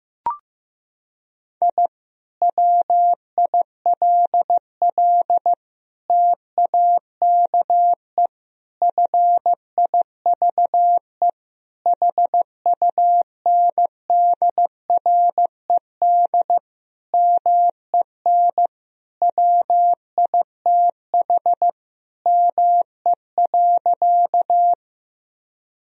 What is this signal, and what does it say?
Morse code, heard as a keyed tone, about 15 wpm, 700 Hz. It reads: I WILL TAKE FIVE HUNDRED MEN WITH ME.
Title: Sentences from Top 1000 Words - Review Part 2 - 15wpm